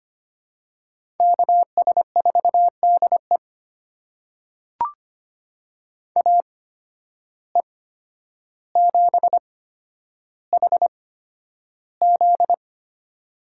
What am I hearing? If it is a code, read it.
Morse code, heard as a keyed tone, about 25 wpm, spaced as for 11 wpm, 700 Hz. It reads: KS4DE AE7HZ
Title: USA Call Signs - 5 Characters - 8x Character Spacing - 25wpm